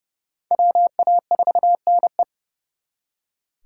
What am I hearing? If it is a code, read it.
Morse code, heard as a keyed tone, about 35 wpm, 700 Hz. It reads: WA4NE
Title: Speed Racing - USA Call Signs 30wpm